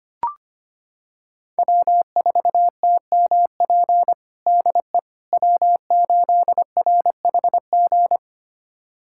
Morse code, heard as a keyed tone, about 25 wpm, 700 Hz. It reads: W4TMP DE W8RHG
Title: USA Callsign DE Callsign - 5 Characters - 25wpm